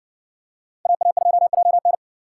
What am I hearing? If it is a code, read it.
Morse code, heard as a keyed tone, about 60 wpm, 700 Hz. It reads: AA2JN